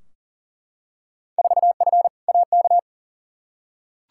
Morse code, heard as a keyed tone, about 40 wpm, 700 Hz. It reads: 4F AK
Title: Field Day Exchanges Encore - 40wpm